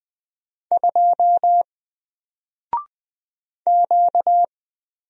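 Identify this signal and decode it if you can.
Morse code, heard as a keyed tone, about 20 wpm, 700 Hz. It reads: 2 Q
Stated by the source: Single letter and number Morse drill at 20 wpm, from N4LCD.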